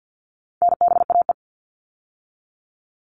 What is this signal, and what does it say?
Morse code, heard as a keyed tone, about 50 wpm, 700 Hz. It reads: NBAE
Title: All Cut Numbers - 4-Digit 50wpm